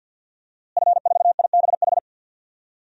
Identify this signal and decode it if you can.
Morse code, heard as a keyed tone, about 47 wpm, 700 Hz. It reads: F4IBH
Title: DXCC-100 Call Signs - 5 Characters - Repeated 2x - 50wpm